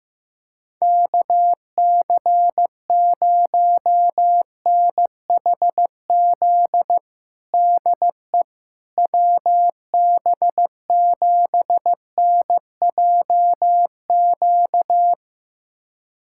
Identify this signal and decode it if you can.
Morse code, heard as a keyed tone, about 15 wpm, 700 Hz. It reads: KC0NHZ DE WB7NJQ
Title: USA Callsign DE Callsign - 6 Characters - 15wpm